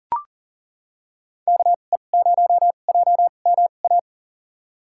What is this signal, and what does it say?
Morse code, heard as a keyed tone, about 40 wpm, spaced as for 29 wpm, 700 Hz. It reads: KE0JMA
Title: USA Call Signs - 6 Characters - 2x Character Spacing - 40wpm